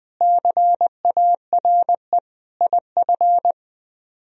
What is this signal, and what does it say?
Morse code, heard as a keyed tone, about 20 wpm, 700 Hz. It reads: CARE IF